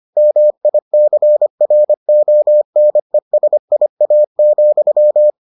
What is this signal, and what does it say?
Morse code, heard as a keyed tone, about 25 wpm, 600 Hz. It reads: MICRONESIA,